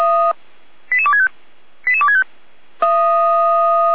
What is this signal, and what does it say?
Five-tone selective call (ZVEI), heard as quick runs of five tones, tones of about 80 ms. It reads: 88265 88265